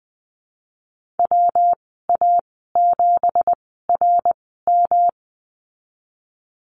Morse code, heard as a keyed tone, about 20 wpm, spaced as for 15 wpm, 700 Hz. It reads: WA7RM
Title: 179 Lesson 29g - Slash - Call Signs 20wpm